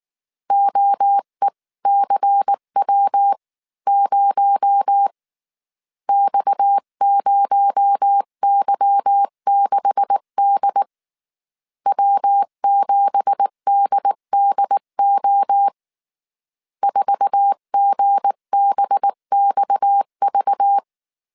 Morse code, heard as a keyed tone, about 19 wpm, 800 Hz. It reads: OECW0 X0Y6D W7DDO 4GBXV
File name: CW / DET / 1 9 1500C S B 18wpm 2551